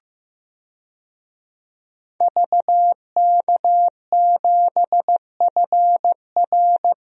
Morse code, heard as a keyed tone, about 15 wpm, 700 Hz. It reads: VK7FR
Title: DXCC-100 Call Signs - 5 Characters - 15wpm